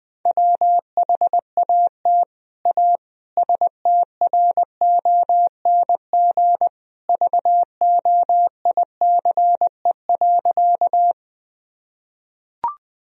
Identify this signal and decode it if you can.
Morse code, heard as a keyed tone, about 20 wpm, 700 Hz. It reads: WHAT A STRONG VOICE.
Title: Sentences from Top 1000 Words - Review Part 3 - 20wpm